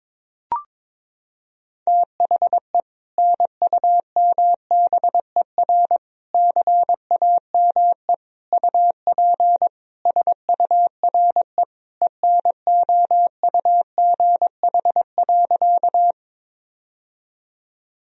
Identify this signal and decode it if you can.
Morse code, heard as a keyed tone, about 22 wpm, 700 Hz. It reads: THE NUMBER CAME UP SURE ENOUGH.
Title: Sentences from Top 1000 Words - Review Part 1 - 22wpm